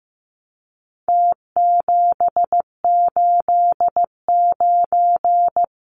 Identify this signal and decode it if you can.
Morse code, heard as a keyed tone, about 15 wpm, 700 Hz. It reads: T789
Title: Numbers - Contest Style - 4-Digit 15wpm